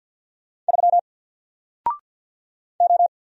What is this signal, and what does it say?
Morse code, heard as a keyed tone, about 48 wpm, 700 Hz. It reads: 3 O
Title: Single Letter-Number 50wpm